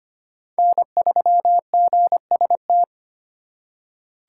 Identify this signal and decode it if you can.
Morse code, heard as a keyed tone, about 25 wpm, 700 Hz. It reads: N3GST
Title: USA Call Signs - 5 Characters - 4x Character Spacing - 25wpm